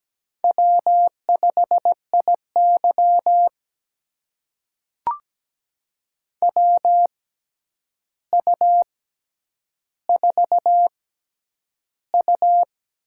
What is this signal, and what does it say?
Morse code, heard as a keyed tone, about 17 wpm, spaced as for 9 wpm, 700 Hz. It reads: W5IY WU4U
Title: USA Call Signs - 4 Characters - 6x Character Spacing - 17wpm